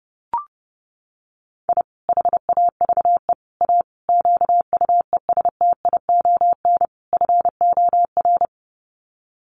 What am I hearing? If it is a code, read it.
Morse code, heard as a keyed tone, about 30 wpm, 700 Hz. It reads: I HAVE A QUESTION FOR
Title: Sets of 5 Words Encore2 30wpm